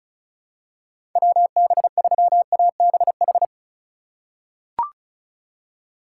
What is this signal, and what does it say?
Morse code, heard as a keyed tone, about 36 wpm, 700 Hz. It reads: WB3ABH
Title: USA Call Signs - 6 Characters - 3x Character Spacing - 35wpm